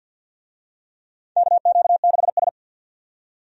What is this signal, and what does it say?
Morse code, heard as a keyed tone, about 52 wpm, 700 Hz. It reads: KQ6S
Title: USA Call Signs - 4 Characters - 50wpm